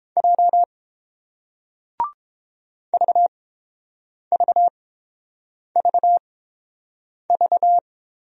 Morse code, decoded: J VVVV

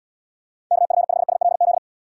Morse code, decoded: KF5SLZ

